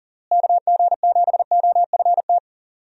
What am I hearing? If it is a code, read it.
Morse code, heard as a keyed tone, about 37 wpm, 700 Hz. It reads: KG7OFT
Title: USA Call Signs - 6 Characters - Repeated 3x - 40wpm